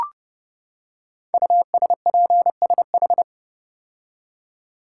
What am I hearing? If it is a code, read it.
Morse code, heard as a keyed tone, about 30 wpm, 700 Hz. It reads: USPSH
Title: Code Groups - 5 Letters - 30wpm